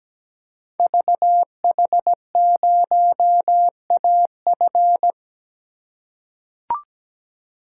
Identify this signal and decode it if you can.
Morse code, heard as a keyed tone, about 17 wpm, 700 Hz. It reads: VH0AF
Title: Code Groups - 5 Letter:Numbers - 17wpm